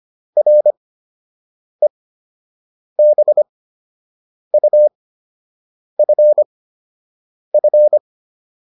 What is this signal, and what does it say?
Morse code, heard as a keyed tone, about 25 wpm, spaced as for 7 wpm, 600 Hz. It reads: REBUFF